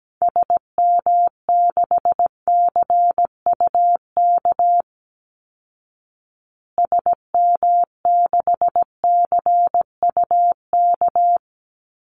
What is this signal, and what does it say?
Morse code, heard as a keyed tone, about 17 wpm, 700 Hz. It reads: SM6CUK SM6CUK
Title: Call Signs Repeated 2x - 17wpm